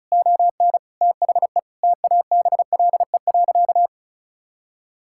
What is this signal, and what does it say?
Morse code, heard as a keyed tone, about 34 wpm, 700 Hz. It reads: ON THE TABLE.
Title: Sentences from Top 1000 Words - Review Part 6 - 35wpm